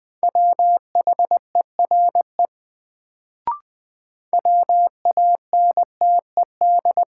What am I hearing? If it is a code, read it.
Morse code, heard as a keyed tone, about 20 wpm, 700 Hz. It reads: WHERE WANTED
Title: Sets of 2 Words - New Word Drill - 20wpm